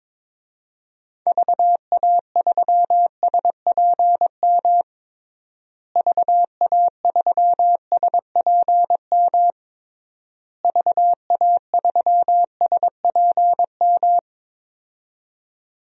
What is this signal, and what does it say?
Morse code, heard as a keyed tone, about 22 wpm, 700 Hz. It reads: VA3SPM VA3SPM VA3SPM
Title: DXCC-100 Call Signs - 6 Characters - Repeated 3x - 22wpm